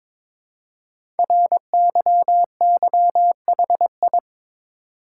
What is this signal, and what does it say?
Morse code, heard as a keyed tone, about 22 wpm, 700 Hz. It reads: RYYHI
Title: Code Groups - 5 Letters - 22wpm